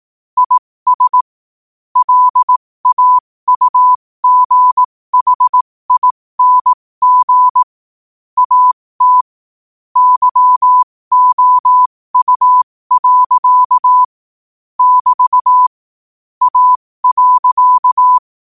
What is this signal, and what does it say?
Morse code, heard as a keyed tone, about 18 wpm, spaced as for 15 wpm, 1000 Hz. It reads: IS LAUGHING AT YOU. <BT> A.
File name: qotdfiles / 15wpm